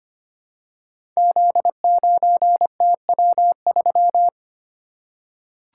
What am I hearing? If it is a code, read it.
Morse code, heard as a keyed tone, about 25 wpm, 700 Hz. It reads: Z9TW3
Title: Code Groups - 5 Letter:Numbers - 25wpm